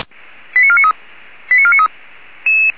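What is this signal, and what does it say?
Five-tone selective call (ZVEI), heard as quick runs of five tones, tones of about 70 ms. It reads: 89493 89493
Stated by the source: ZVEI five-tone call received on SDR.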